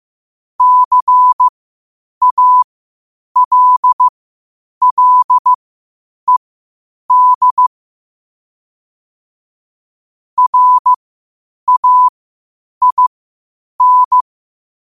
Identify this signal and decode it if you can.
Morse code, heard as a keyed tone, about 15 wpm, spaced as for 8 wpm, 1000 Hz. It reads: CALLED RAIN
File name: qotdfiles / 8wpm